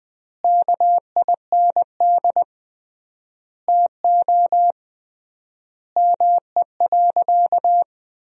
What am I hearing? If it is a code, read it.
Morse code, heard as a keyed tone, about 20 wpm, 700 Hz. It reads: KIND TO ME.
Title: 3x Word-Spacing - Sentences from Top 200 Words - 20wpm